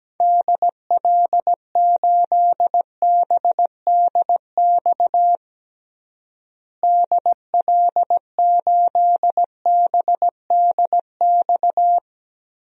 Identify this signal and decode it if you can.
Morse code, heard as a keyed tone, about 17 wpm, 700 Hz. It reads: DL8BDX DL8BDX
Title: DXCC-100 Call Signs - 6 Characters - Repeated 3x - 17wpm